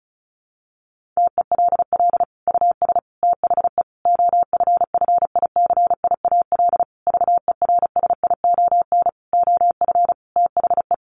Morse code, heard as a keyed tone, about 34 wpm, 700 Hz. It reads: TELL US THE OFFICIAL VERSION OF THE
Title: Sentences from Top 1400 Words 35wpm